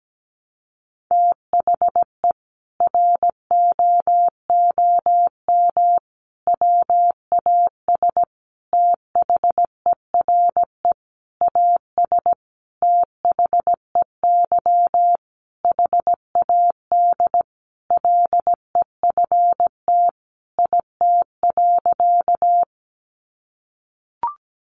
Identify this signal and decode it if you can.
Morse code, heard as a keyed tone, about 17 wpm, 700 Hz. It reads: THE ROOM WAS THERE AS THEY HAD LEFT IT.
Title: A Sound Of Thunder 17wpm